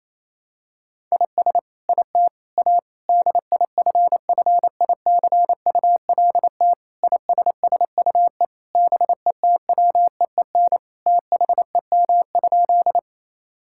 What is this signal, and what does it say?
Morse code, heard as a keyed tone, about 27 wpm, 700 Hz. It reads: IS IT A DIFFICULT ISSUE BETWEEN THEM?
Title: Sentences from Top 1000 Words - Review Part 6 - 28wpm